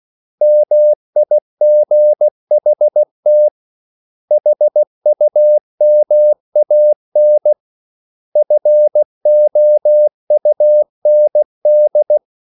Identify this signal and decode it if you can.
Morse code, heard as a keyed tone, about 16 wpm, 600 Hz. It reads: MIGHT HUMAN FOUND
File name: CW / Practice / 500MostCommonEnglishWords16